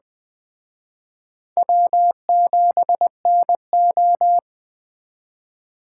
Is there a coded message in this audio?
Yes: Morse code from 1.6 to 4.4 s, heard as a keyed tone, about 20 wpm, 700 Hz. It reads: W7NO